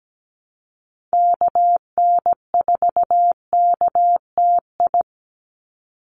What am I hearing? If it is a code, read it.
Morse code, heard as a keyed tone, about 17 wpm, 700 Hz. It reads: KN4KTI
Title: USA Call Signs 17wpm